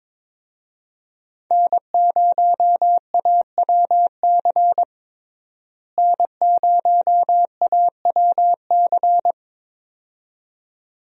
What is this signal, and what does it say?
Morse code, heard as a keyed tone, about 22 wpm, 700 Hz. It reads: N0AWC N0AWC
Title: USA Call Signs - 5 Characters - Repeated 2x - 22wpm